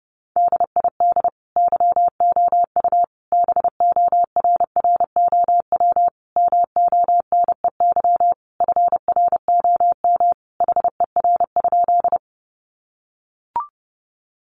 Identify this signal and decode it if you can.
Morse code, heard as a keyed tone, about 30 wpm, 700 Hz. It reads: DID YOU BORROW MONEY FROM HER?
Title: Sentences from Top 1500 Words 30wpm